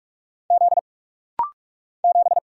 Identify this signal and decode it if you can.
Morse code, heard as a keyed tone, about 44 wpm, 700 Hz. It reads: Z 7
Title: Single Letter-Number - Rapid-Fire 45wpm